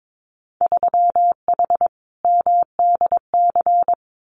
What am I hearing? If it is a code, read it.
Morse code, heard as a keyed tone, about 22 wpm, 700 Hz. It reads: 3H MDC
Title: Winter Field Day Exchanges Encore 22wpm